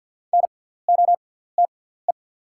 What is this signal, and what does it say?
Morse code, heard as a keyed tone, about 48 wpm, spaced as for 17 wpm, 700 Hz. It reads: NOTE
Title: 6x Character Spacing - Top 500 Words - 50wpm